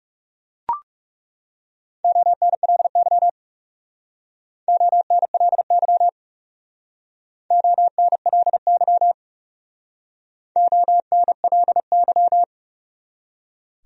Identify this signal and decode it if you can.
Morse code, heard as a keyed tone, about 37 wpm, 700 Hz. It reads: ONLY ONLY ONLY ONLY
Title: Speed-Racing Top 200 Words 30wpm